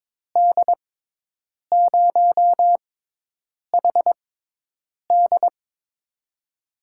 Morse code, heard as a keyed tone, about 22 wpm, spaced as for 8 wpm, 700 Hz. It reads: D0HD
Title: USA Call Signs - 5 Characters - 6x Character Spacing - 22wpm